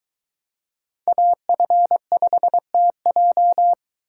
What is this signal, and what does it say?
Morse code, heard as a keyed tone, about 22 wpm, 700 Hz. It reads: AF5TJ